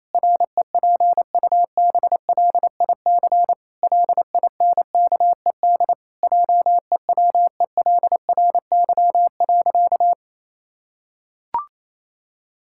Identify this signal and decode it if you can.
Morse code, heard as a keyed tone, about 28 wpm, 700 Hz. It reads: REPUBLIC LINKED JEWELRY.
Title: Pangram Challenge v2 - 28wpm